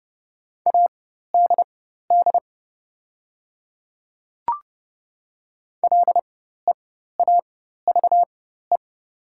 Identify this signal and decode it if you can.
Morse code, heard as a keyed tone, about 31 wpm, spaced as for 14 wpm, 690 Hz. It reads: ADD LEAVE
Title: 4x Character Spacing - Top 500 Words - 30wpm